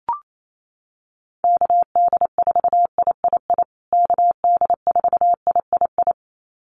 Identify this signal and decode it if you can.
Morse code, heard as a keyed tone, about 28 wpm, 700 Hz. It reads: KD4III KD4III
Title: USA Call Signs Repeated 2x 28wpm